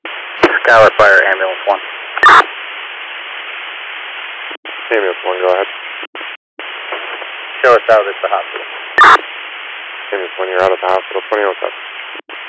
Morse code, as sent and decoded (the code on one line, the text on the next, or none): none